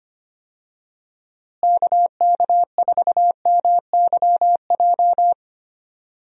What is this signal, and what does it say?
Morse code, heard as a keyed tone, about 25 wpm, 700 Hz. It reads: KK4MYJ
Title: USA Call Signs - 6 Characters - Repeated 2x - 25wpm